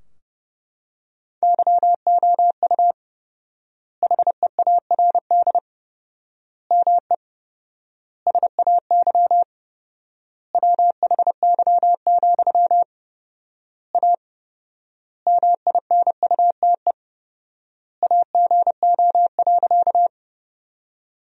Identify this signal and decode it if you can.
Morse code, heard as a keyed tone, about 30 wpm, 700 Hz. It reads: YOU HEARD ME SAY WHY, A MINUTE AGO.
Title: 4x Word-Spacing - Sentences from Top 500 Words - 30wpm